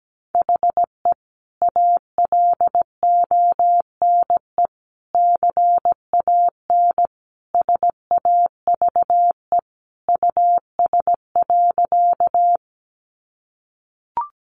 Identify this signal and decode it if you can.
Morse code, heard as a keyed tone, about 17 wpm, 700 Hz. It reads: HE ALONE CAN SAVE US.
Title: Sentences from Top 1100 Words 17wpm